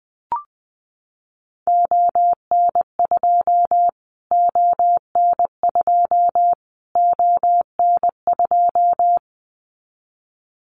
Morse code, heard as a keyed tone, about 20 wpm, 700 Hz. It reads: ON2 ON2 ON2
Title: Callsign Prefixes for Top 25 DXCC - Repeated 3x - 20wpm